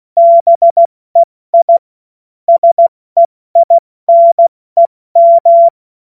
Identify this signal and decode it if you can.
Morse code, heard as a keyed tone, about 16 wpm, 680 Hz. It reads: BEI SEINEM
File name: Morsekurs / Witze / Witz 31.Oktober 2023 70 F80